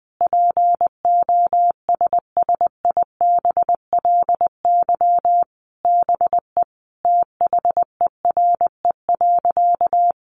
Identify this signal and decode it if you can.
Morse code, heard as a keyed tone, about 20 wpm, 700 Hz. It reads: POSSIBLY BE THERE.
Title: Sentences from Top 1300 Words 20wpm